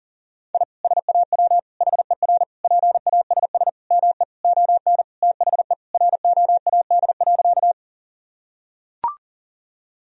Morse code, heard as a keyed tone, about 37 wpm, 700 Hz. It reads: I SAW HER PASS ME ON THE ROAD.
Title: Sentences from Top 1000 Words - Review Part 2 - 40wpm